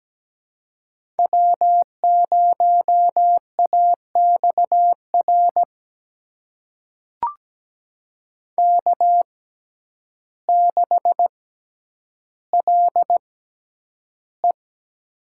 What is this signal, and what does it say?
Morse code, heard as a keyed tone, about 17 wpm, spaced as for 9 wpm, 700 Hz. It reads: W0AXR K6LE
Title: USA Call Signs - 6x Character Spacing - 17wpm